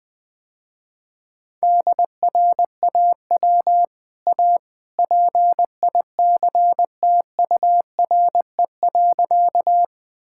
Morse code, heard as a keyed tone, about 20 wpm, 700 Hz. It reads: DRAW A PICTURE.
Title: Easy Sentences with Hints - 20wpm